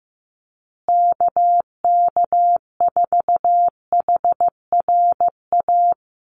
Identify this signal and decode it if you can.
Morse code, heard as a keyed tone, about 15 wpm, 700 Hz. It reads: KK4HRA
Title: USA Call Signs Repeated 2x 15wpm